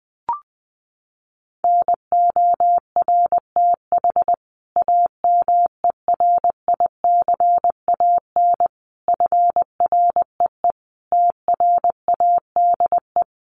Morse code, heard as a keyed tone, about 20 wpm, 700 Hz. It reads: NORTH AMERICAN FREE TRADE
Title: Sets of 5 Words Encore 20wpm